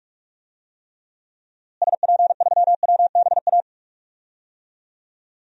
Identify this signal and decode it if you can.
Morse code, heard as a keyed tone, about 46 wpm, 700 Hz. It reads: SP3WBA
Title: DXCC-100 Call Signs - 6 Characters - Repeated 2x - 45wpm